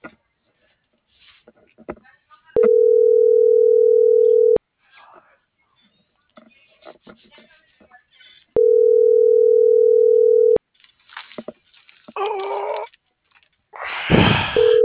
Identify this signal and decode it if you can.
Ringback tone (North America), 440 + 480 Hz.